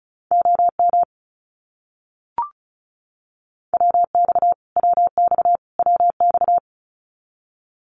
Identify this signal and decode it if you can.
Morse code, heard as a keyed tone, about 34 wpm, 700 Hz. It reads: OM WX WX WX
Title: Abbreviations Repeated 3x - 35wpm